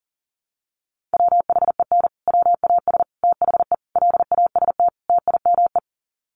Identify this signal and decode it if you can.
Morse code, heard as a keyed tone, about 40 wpm, 700 Hz. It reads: WHEN WAS THE LAST TIME
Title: Sets of 5 Words Encore 40wpm